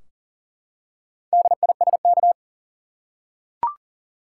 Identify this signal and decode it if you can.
Morse code, heard as a keyed tone, about 40 wpm, 700 Hz. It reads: DISK
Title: Four Letter Words - 40wpm